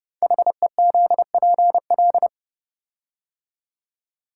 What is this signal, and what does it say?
Morse code, heard as a keyed tone, about 30 wpm, 700 Hz. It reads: HEZPL
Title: Code Groups - 5 Letters - 30wpm